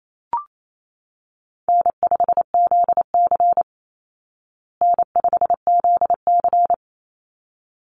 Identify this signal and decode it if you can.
Morse code, heard as a keyed tone, about 28 wpm, 700 Hz. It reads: N5ZC N5ZC
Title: Call Signs Repeated 3x - 28wpm